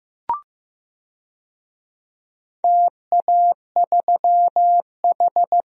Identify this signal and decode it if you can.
Morse code, heard as a keyed tone, about 15 wpm, 700 Hz. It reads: TA3H
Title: DXCC-100 Call Signs - 4 Characters - Repeated 2x - 15wpm